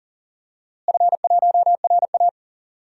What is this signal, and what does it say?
Morse code, heard as a keyed tone, about 40 wpm, 700 Hz. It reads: F1RA
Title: DXCC-100 Call Signs - 4 Characters - Repeated 2x - 40wpm